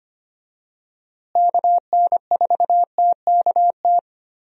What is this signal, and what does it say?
Morse code, heard as a keyed tone, about 25 wpm, 700 Hz. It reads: KN4TKT
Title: USA Call Signs - 6 Characters - Repeated 3x - 25wpm